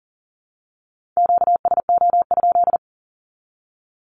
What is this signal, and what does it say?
Morse code, heard as a keyed tone, about 40 wpm, 700 Hz. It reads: QSO?
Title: Q-Codes 40wpm